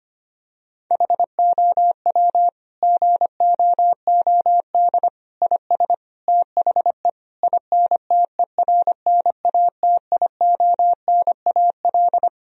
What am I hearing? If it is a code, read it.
Morse code, heard as a keyed tone, about 25 wpm, 700 Hz. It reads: HOW GOOD IS THE INTERNATIONAL
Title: Sentences from Top 1000 Words - Review Part 4 - 25wpm